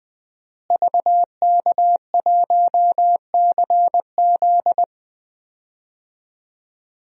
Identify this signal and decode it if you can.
Morse code, heard as a keyed tone, about 20 wpm, 700 Hz. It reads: VK1CZ
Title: DXCC-100 Call Signs - 5 Characters - Repeated 3x - 20wpm